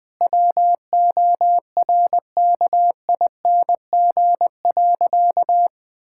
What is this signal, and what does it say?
Morse code, heard as a keyed tone, about 21 wpm, 700 Hz. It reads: WORKING.
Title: Sentences with 3-Word Phrases - 20wpm